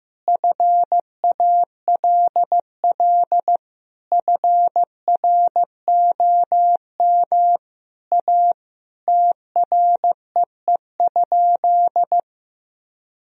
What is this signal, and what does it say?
Morse code, heard as a keyed tone, about 15 wpm, 700 Hz. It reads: FALL FROM A TREE?
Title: Sentences from Top 1000 Words - Review Part 2 - 15wpm